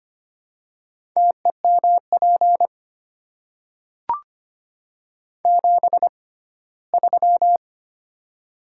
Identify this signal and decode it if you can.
Morse code, heard as a keyed tone, about 25 wpm, spaced as for 17 wpm, 700 Hz. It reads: TEMP 73
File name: Top 90 QSO Elements - 6x Character Spacing - 25wpm